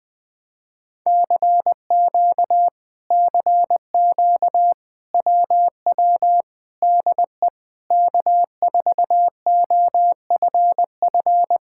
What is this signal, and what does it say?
Morse code, heard as a keyed tone, about 20 wpm, 700 Hz. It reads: CQ CQ WW DE K4OFF
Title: Speed-Racing Common QSO Phrases - 20wpm